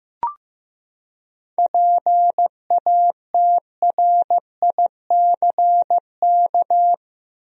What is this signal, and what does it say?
Morse code, heard as a keyed tone, about 15 wpm, 700 Hz. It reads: PATRICK
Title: USA Common Names - 15wpm